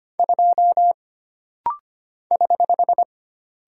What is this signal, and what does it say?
Morse code, heard as a keyed tone, about 25 wpm, 700 Hz. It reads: 2 <HH>
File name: Single Character (Letter:Number:Punctuation:Prosign) - Rapid-Fire - 25wpm